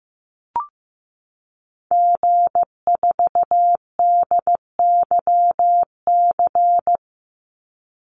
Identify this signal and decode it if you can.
Morse code, heard as a keyed tone, about 15 wpm, 700 Hz. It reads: G4DYC
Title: Call Signs Repeated 2x - 15wpm